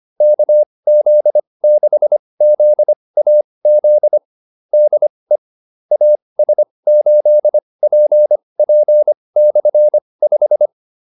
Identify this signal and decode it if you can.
Morse code, heard as a keyed tone, about 25 wpm, spaced as for 20 wpm, 600 Hz. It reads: KZ6ZAZ DE AS8PP/5